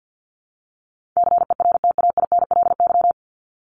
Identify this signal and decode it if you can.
Morse code, heard as a keyed tone, about 54 wpm, 700 Hz. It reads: CERTAINLY
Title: Top 751-1000 Words 50wpm